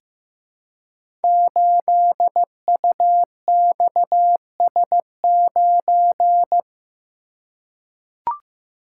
Morse code, heard as a keyed tone, about 15 wpm, 700 Hz. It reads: 8UXS9
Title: Code Groups - 5 Letter:Numbers - 15wpm